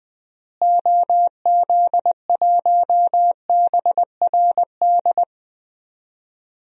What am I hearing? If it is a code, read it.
Morse code, heard as a keyed tone, about 21 wpm, 700 Hz. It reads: OZ1BRD
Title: DXCC-100 Call Signs - 6 Characters - Repeated 3x - 20wpm